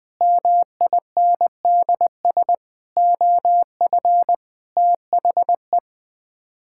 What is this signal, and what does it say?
Morse code, heard as a keyed tone, about 21 wpm, 700 Hz. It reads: MINDS OF THE
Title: Sets of 8 Words - 20wpm